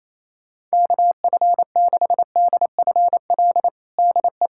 Morse code, heard as a keyed tone, about 28 wpm, 700 Hz. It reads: KF6DFL DE
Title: USA Callsign DE Callsign - 6 Characters - 28wpm